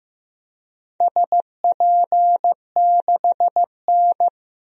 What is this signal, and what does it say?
Morse code, heard as a keyed tone, about 15 wpm, 700 Hz. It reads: SP6N
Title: DXCC-100 Call Signs - 4 Characters - 15wpm